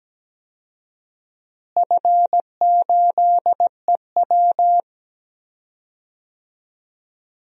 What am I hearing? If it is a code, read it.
Morse code, heard as a keyed tone, about 17 wpm, 700 Hz. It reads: F8EW
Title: DXCC-100 Call Signs - 4 Characters - Repeated 3x - 17wpm